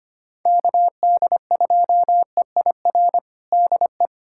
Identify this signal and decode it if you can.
Morse code, heard as a keyed tone, about 25 wpm, 700 Hz. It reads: KD2EIR DE
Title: USA Callsign DE Callsign - 6 Characters - 25wpm